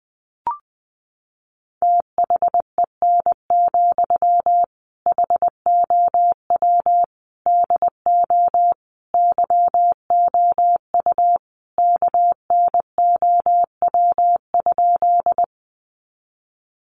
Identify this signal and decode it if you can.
Morse code, heard as a keyed tone, about 20 wpm, 700 Hz. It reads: THEN, HOW DO YOU KNOW?
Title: Sentences from Top 100 Words 20wpm